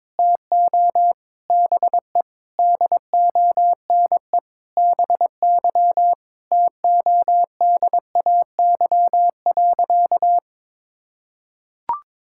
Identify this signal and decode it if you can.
Morse code, heard as a keyed tone, about 22 wpm, 700 Hz. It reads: TO BE DONE BY TODAY.
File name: practice / 22wpm / Sentences with 4-Word Phrases - 22wpm